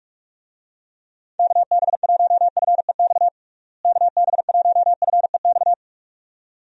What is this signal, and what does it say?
Morse code, heard as a keyed tone, about 46 wpm, 700 Hz. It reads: KB1FEX KB1FEX
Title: USA Call Signs - 6 Characters - Repeated 2x - 45wpm